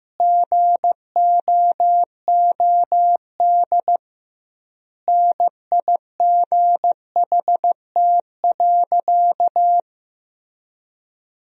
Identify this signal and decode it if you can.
Morse code, heard as a keyed tone, about 15 wpm, 700 Hz. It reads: GOOD NIGHT.
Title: 2x Word-Spacing - Sentences from Top 300 Words - 15wpm